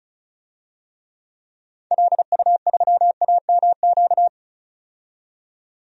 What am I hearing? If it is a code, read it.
Morse code, heard as a keyed tone, about 36 wpm, 700 Hz. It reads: LU3AMQ
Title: DXCC-100 Call Signs - 6 Characters - 35wpm